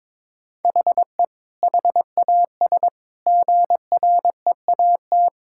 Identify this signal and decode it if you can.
Morse code, heard as a keyed tone, about 22 wpm, 700 Hz. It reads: HE HAS GREAT
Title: Sentences from Top 1000 Words - Review Part 6 - 22wpm